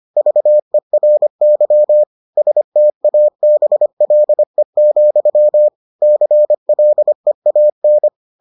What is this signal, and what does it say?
Morse code, heard as a keyed tone, about 25 wpm, 600 Hz. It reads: VERY STABLE, CLEAN